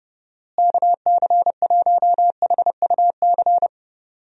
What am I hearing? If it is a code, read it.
Morse code, heard as a keyed tone, about 30 wpm, 700 Hz. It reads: KC1HUC
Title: USA Call Signs - 6 Characters - Repeated 2x - 30wpm